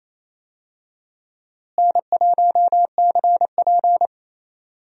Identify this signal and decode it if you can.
Morse code, heard as a keyed tone, about 28 wpm, 700 Hz. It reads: N1CP
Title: USA Call Signs - 4 Characters - Repeated 2x - 28wpm